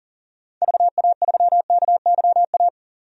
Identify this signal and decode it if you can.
Morse code, heard as a keyed tone, about 40 wpm, 700 Hz. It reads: VA3KYA
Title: DXCC-100 Call Signs - 6 Characters - 40wpm